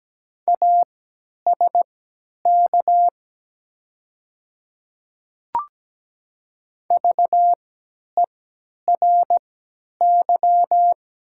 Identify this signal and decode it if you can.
Morse code, heard as a keyed tone, about 17 wpm, spaced as for 10 wpm, 700 Hz. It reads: ASK VERY
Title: 3x Character Spacing - Top 300 Words - 17wpm